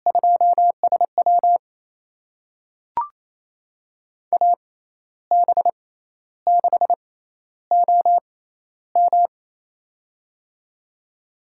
Morse code, heard as a keyed tone, about 27 wpm, spaced as for 12 wpm, 700 Hz. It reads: 2SW AB6OM
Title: USA Call Signs - 5 Characters - 6x Character Spacing - 28wpm